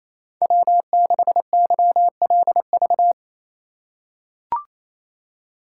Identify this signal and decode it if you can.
Morse code, heard as a keyed tone, about 28 wpm, 700 Hz. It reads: W6YLV
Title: USA Call Signs - 5 Characters - 4x Character Spacing - 28wpm